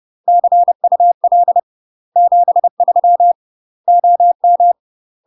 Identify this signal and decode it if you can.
Morse code, heard as a keyed tone, about 30 wpm, 700 Hz. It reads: CUL 73 OM
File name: UK MASTER QSOs TRAINING / 30WPM / UK MASTER QSOs 0005